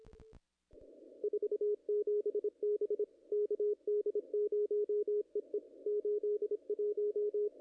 Morse code, heard as a keyed tone, about 27 wpm, 420 Hz, with radio noise. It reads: 47 B KD0EE 81
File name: contests / ss-cw-k5tr-op-n5rz -2300